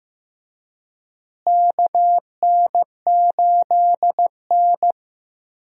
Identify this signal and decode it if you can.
Morse code, heard as a keyed tone, about 15 wpm, 700 Hz. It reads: KN8N
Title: USA Call Signs - 4 Characters - Repeated 2x - 15wpm